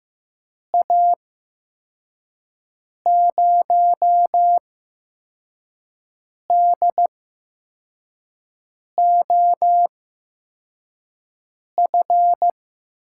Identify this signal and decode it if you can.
Morse code, heard as a keyed tone, about 15 wpm, spaced as for 4 wpm, 700 Hz. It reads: A0DOF
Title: USA Call Signs - 6 Characters - 8x Character Spacing - 15wpm